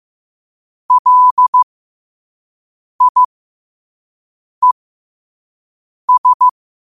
Morse code, heard as a keyed tone, about 15 wpm, spaced as for 5 wpm, 1000 Hz. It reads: LIES
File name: qotdfiles / 5wpm